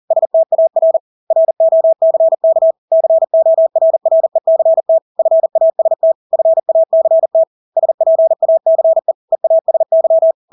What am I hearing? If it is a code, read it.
Morse code, heard as a keyed tone, about 40 wpm, 650 Hz. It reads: STAR ROCK CORRECT FAST FACT SPACE EASY
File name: Engelske ord 40-40 wpm